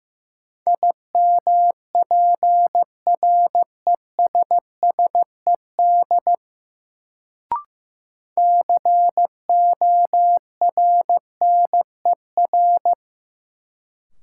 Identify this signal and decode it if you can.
Morse code, heard as a keyed tone, about 15 wpm, 700 Hz. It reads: IMPRESSED CORNER
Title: Sentences from Top 1600 Words - New Word Drill - 15wpm